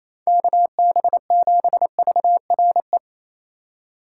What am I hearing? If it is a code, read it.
Morse code, heard as a keyed tone, about 28 wpm, 700 Hz. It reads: KB7VRE
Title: USA Call Signs - 6 Characters - 3x Character Spacing - 28wpm